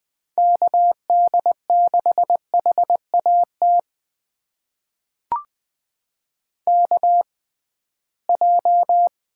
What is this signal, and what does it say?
Morse code, heard as a keyed tone, about 20 wpm, spaced as for 15 wpm, 700 Hz. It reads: KD6HAT KJ
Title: USA Call Signs - 6 Characters - 6x Character Spacing - 20wpm